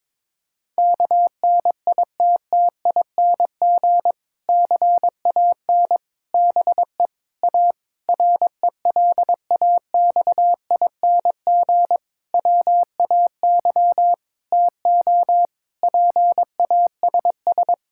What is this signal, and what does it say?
Morse code, heard as a keyed tone, about 22 wpm, 700 Hz. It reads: KNITTING CAN BE A RELAXING WAY TO PASS
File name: Difficult Sentences with Hints - 22wpm